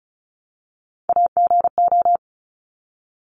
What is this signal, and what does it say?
Morse code, heard as a keyed tone, about 34 wpm, 700 Hz. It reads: AGO
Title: Top 500 Words 35wpm